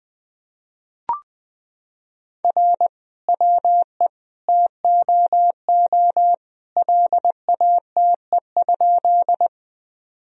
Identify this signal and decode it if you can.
Morse code, heard as a keyed tone, about 20 wpm, 700 Hz. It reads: R WE TOO LATE?
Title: Sentences from Top 300 Words and Abbreviations 20wpm